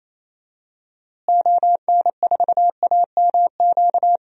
Morse code, heard as a keyed tone, about 28 wpm, 700 Hz. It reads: ON4AMQ
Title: DXCC-100 Call Signs - 6 Characters - Repeated 2x - 28wpm